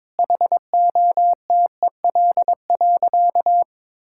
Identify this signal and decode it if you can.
Morse code, heard as a keyed tone, about 22 wpm, 700 Hz. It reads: HOTEL.
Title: Sentences from Top 1600 Words 22wpm